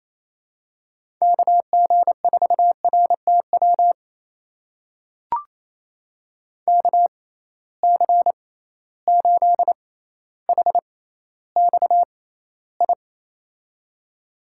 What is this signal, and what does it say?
Morse code, heard as a keyed tone, about 28 wpm, spaced as for 14 wpm, 700 Hz. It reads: KG4RTW KC8HXI
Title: USA Call Signs - 6 Characters - 6x Character Spacing - 28wpm